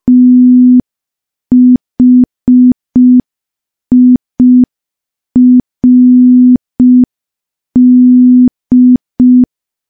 Morse code, heard as a keyed tone, about 5 wpm, 250 Hz. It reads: THIRD
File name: morse / morse-training-05WPM